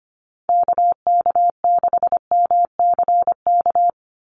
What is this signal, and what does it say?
Morse code, heard as a keyed tone, about 25 wpm, 690 Hz. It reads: KK6MCK
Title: USA Call Signs 25wpm